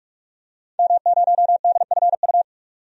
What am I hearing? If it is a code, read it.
Morse code, heard as a keyed tone, about 46 wpm, 700 Hz. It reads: M0DFU
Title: DXCC-100 Call Signs - 5 Characters - 45wpm